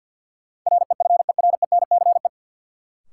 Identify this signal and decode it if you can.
Morse code, heard as a keyed tone, about 50 wpm, 700 Hz. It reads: REFERENCE
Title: Sentences from Top 1200 Words - New Word Drill - 50wpm